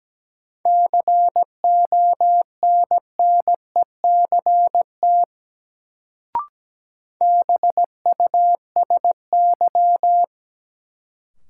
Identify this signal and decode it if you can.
Morse code, heard as a keyed tone, about 17 wpm, 700 Hz. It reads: CONNECT BUSY
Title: Sentences from Top 1500 Words - New Word Drill Alt - 17wpm